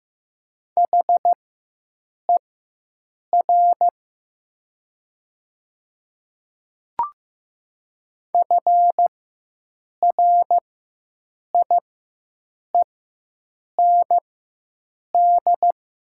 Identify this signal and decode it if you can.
Morse code, heard as a keyed tone, about 15 wpm, spaced as for 7 wpm, 700 Hz. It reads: HER FRIEND
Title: 4x Character Spacing - Top 300 Words - 15wpm